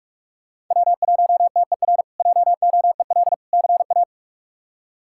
Morse code, heard as a keyed tone, about 46 wpm, 700 Hz. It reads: W1TER JOEL CA
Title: K1USN SST QSO Practice - 45wpm